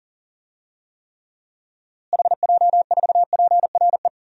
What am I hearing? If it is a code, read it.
Morse code, heard as a keyed tone, about 39 wpm, 700 Hz. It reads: HJ4PRE
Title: DXCC-100 Call Signs - 6 Characters - Repeated 2x - 40wpm